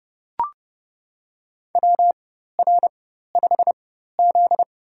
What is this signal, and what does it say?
Morse code, heard as a keyed tone, about 32 wpm, spaced as for 14 wpm, 700 Hz. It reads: WR5Z
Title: USA Call Signs - 4 Characters - 4x Character Spacing - 30wpm